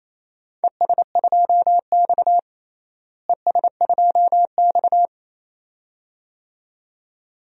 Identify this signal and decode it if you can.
Morse code, heard as a keyed tone, about 28 wpm, 700 Hz. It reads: ES2X ES2X
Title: DXCC-100 Call Signs - 4 Characters - Repeated 2x - 28wpm